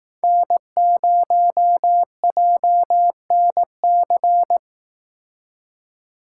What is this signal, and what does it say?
Morse code, heard as a keyed tone, about 19 wpm, 700 Hz. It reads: N0JNC